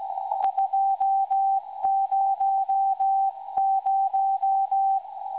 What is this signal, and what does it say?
Morse code, heard as a keyed tone, about 17 wpm, 780 Hz. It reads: 200
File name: WebSDR records / SAQ recording 17.2kHz